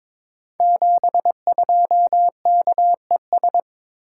Morse code, heard as a keyed tone, about 22 wpm, 700 Hz. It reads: 72KES
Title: Code Groups - 5 Letter:Numbers - 22wpm